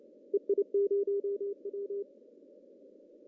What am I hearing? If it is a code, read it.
Morse code, heard as a keyed tone, about 29 wpm, 400 Hz. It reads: EI0W
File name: CW / cqww-cw-k5tr